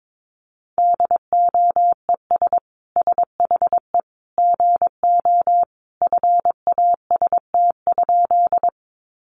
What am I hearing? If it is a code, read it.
Morse code, heard as a keyed tone, about 22 wpm, 700 Hz. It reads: DOES SHE GO FAST?
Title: Sentences from Top 400 Words 22wpm